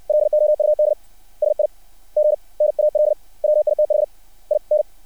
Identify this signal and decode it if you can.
Morse code, heard as a keyed tone, about 60 wpm, spaced as for 39 wpm, 600 Hz. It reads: LOUD SIGNAL WITHIN
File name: archived / CRSnet